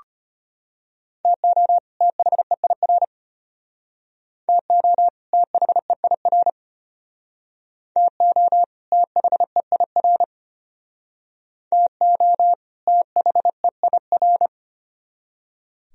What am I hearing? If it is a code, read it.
Morse code, heard as a keyed tone, about 35 wpm, 700 Hz. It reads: TO THEIR TO THEIR TO THEIR TO THEIR